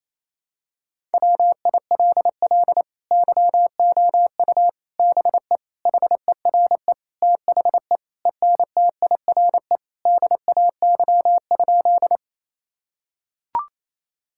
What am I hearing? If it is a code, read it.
Morse code, heard as a keyed tone, about 28 wpm, 700 Hz. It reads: WILL YOU BE HERE THE ENTIRE DAY?
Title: Sentences from Top 1000 Words - Review Part 5 - 28wpm